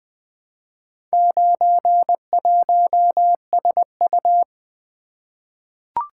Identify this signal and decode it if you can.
Morse code, heard as a keyed tone, about 20 wpm, 700 Hz. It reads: 91SU
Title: Four Letter-Number Combinations - 20wpm